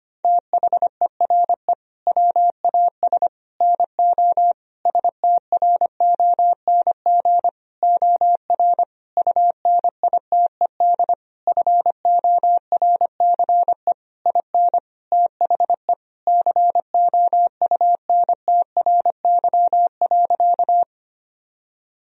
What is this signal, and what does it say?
Morse code, heard as a keyed tone, about 25 wpm, 700 Hz. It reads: THERE WAS NO STRONG OR UNITED FORCE IN THE COUNTRY.
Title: Sentences from Top 1000 Words - Review Part 4 - 25wpm